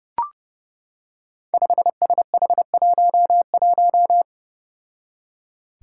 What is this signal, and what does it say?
Morse code, heard as a keyed tone, about 30 wpm, 700 Hz. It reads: 5SH11
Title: Five Challenging Letter-Number Combinations 30wpm